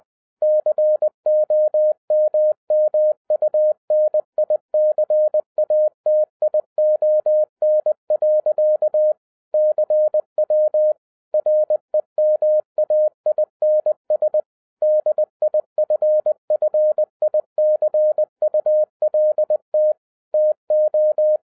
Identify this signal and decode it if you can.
Morse code, heard as a keyed tone, about 20 wpm, 610 Hz. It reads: COMMUNICATION. CW REMAINS DIFFICULT TO
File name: 2012-01-mp3 / n05-CAME BACK TO HAM RADIO AFTER 30 YEARS OF INTERVAL-20wpm